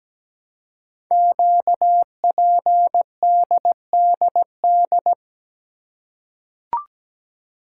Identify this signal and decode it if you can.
Morse code, heard as a keyed tone, about 17 wpm, 700 Hz. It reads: QPDDD
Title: Code Groups - 5 Letters - 17wpm